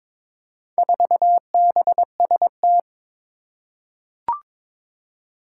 Morse code, heard as a keyed tone, about 22 wpm, 700 Hz. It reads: 4BST
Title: Four Challenging Letter-Number Combinations 22wpm